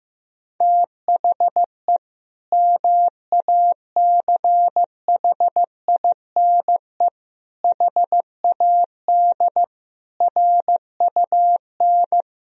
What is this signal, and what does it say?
Morse code, heard as a keyed tone, about 15 wpm, 700 Hz. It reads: THE MACHINE HAD RUN
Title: Sentences from Top 1000 Words - Review Part 2 - 15wpm